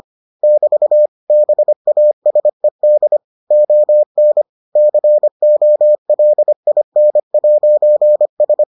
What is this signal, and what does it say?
Morse code, heard as a keyed tone, about 25 wpm, 600 Hz. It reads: <BT> BASED ON COLIN'S